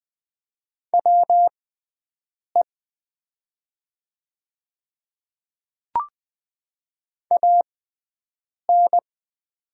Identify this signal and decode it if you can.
Morse code, heard as a keyed tone, about 20 wpm, spaced as for 7 wpm, 690 Hz. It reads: WE AN
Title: 6x Character Spacing - Top 100 Words - 20wpm